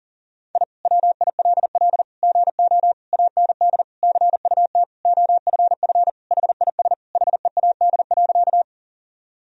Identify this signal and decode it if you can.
Morse code, heard as a keyed tone, about 40 wpm, 700 Hz. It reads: I WILL GO AND CUT OFF HIS HEAD.
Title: Sentences from Top 1000 Words - Review Part 1 - 40wpm